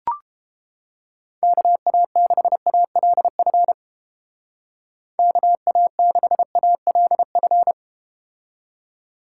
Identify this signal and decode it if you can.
Morse code, heard as a keyed tone, about 31 wpm, 700 Hz. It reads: KA6ALF KA6ALF